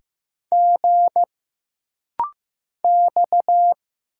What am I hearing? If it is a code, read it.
Morse code, heard as a keyed tone, about 15 wpm, 700 Hz. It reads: G X
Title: Single Letters - Rapid-Fire 15wpm